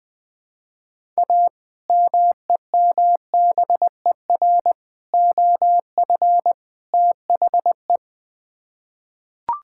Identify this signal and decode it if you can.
Morse code, heard as a keyed tone, about 20 wpm, 700 Hz. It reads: A MEMBER OF THE
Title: Build A Phrase - 4-Word Phrases - 20wpm